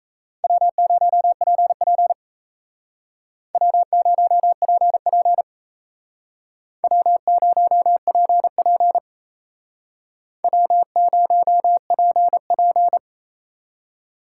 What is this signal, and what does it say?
Morse code, heard as a keyed tone, about 36 wpm, 700 Hz. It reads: W0PP W0PP W0PP W0PP